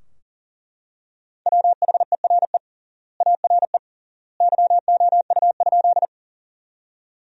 Morse code, heard as a keyed tone, about 40 wpm, 700 Hz. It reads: WHERE ARE YOU?
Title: 3x Word-Spacing - Sentences from Top 200 Words - 40wpm